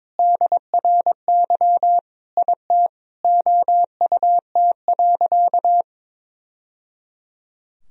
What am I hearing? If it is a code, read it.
Morse code, heard as a keyed tone, about 22 wpm, 700 Hz. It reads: DRY IT OUT.
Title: Sentences from Top 1000 Words - Review Part 3 - 22wpm